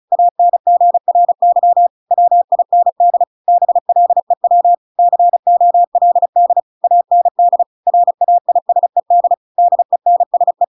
Morse code, heard as a keyed tone, about 34 wpm, 700 Hz. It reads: ANGRY WIND BLEW COLD AND RAISED DENSE
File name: Pinocchio / Pinocchio ch6 35 35 ik1hge